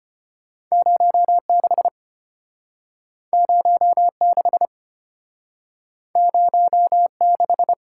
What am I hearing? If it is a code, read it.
Morse code, heard as a keyed tone, about 34 wpm, 700 Hz. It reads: 06 06 06